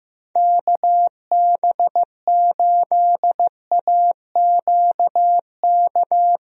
Morse code, heard as a keyed tone, about 15 wpm, 700 Hz. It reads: KB8AQK
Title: USA Call Signs - 6 Characters - Repeated 3x - 15wpm